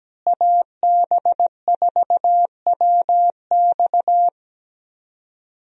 Morse code, heard as a keyed tone, about 17 wpm, 700 Hz. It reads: AB4WX
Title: USA Call Signs - 5 Characters - Repeated 3x - 17wpm